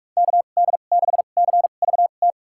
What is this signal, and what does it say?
Morse code, heard as a keyed tone, about 46 wpm, spaced as for 33 wpm, 700 Hz. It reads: KD6CVT
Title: USA Call Signs - 6 Characters - 2x Character Spacing - 45wpm